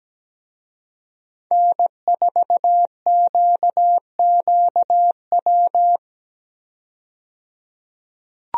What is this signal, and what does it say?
Morse code, heard as a keyed tone, about 17 wpm, 700 Hz. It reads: N4QQW
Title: USA Call Signs - 5 Characters - 17wpm